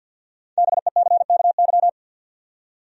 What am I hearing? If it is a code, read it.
Morse code, heard as a keyed tone, about 52 wpm, 700 Hz. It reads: BECKY
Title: USA Common Names - 50wpm